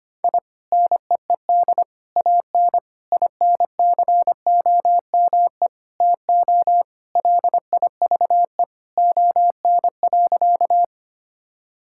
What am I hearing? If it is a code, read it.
Morse code, heard as a keyed tone, about 25 wpm, 700 Hz. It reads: I NEED AN INCOME TO LIVE ON.
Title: Sentences from Top 1000 Words - Review Part 5 - 25wpm